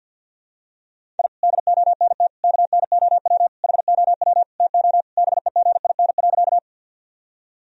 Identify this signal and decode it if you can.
Morse code, heard as a keyed tone, about 50 wpm, 700 Hz. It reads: I DONT KNOW HOW TO BEGIN.
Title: Sentences from Top 1000 Words - Review Part 1 - 50wpm